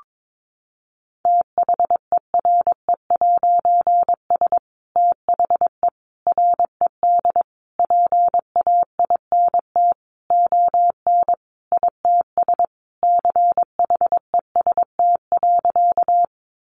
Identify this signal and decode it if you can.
Morse code, heard as a keyed tone, about 22 wpm, 700 Hz. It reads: THERE'S THE RED PAINT ON ITS CHEST.
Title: A Sound Of Thunder 22wpm